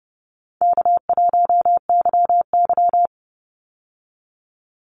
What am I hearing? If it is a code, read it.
Morse code, heard as a keyed tone, about 30 wpm, 700 Hz. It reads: K1YY
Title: Easy Call Signs Repeated 2x - 30wpm